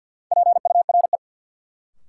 Morse code, heard as a keyed tone, about 51 wpm, 700 Hz. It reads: PURE
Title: Sentences from Top 1500 Words - New Word Drill Alt - 50wpm